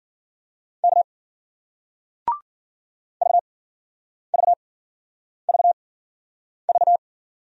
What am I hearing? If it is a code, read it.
Morse code, heard as a keyed tone, about 56 wpm, spaced as for 9 wpm, 700 Hz. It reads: K VVVV